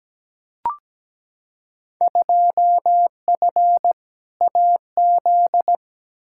Morse code, heard as a keyed tone, about 17 wpm, 700 Hz. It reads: 2F AZ
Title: Field Day Exchanges - 17wpm